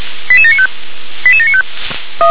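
Five-tone selective call (ZVEI), heard as quick runs of five tones, tones of about 80 ms. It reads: 88795 88795
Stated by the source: ZVEI selective call received on an SDR.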